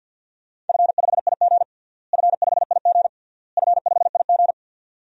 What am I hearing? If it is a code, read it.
Morse code, heard as a keyed tone, about 52 wpm, 700 Hz. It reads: F5IG F5IG F5IG
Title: DXCC-100 Call Signs - 4 Characters - Repeated 3x - 50wpm